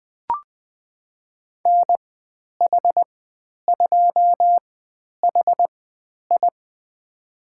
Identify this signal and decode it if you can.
Morse code, heard as a keyed tone, about 20 wpm, spaced as for 10 wpm, 700 Hz. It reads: NH2HI
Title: 095 Lesson 16g - Letter c - Call Signs 20wpm